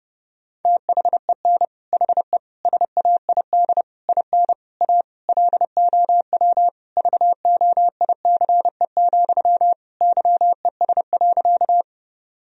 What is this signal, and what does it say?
Morse code, heard as a keyed tone, about 30 wpm, 700 Hz. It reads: THEN HE SAID IN A LOW VOICE, YES.
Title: Sentences from Top 1000 Words - Review Part 1 - 30wpm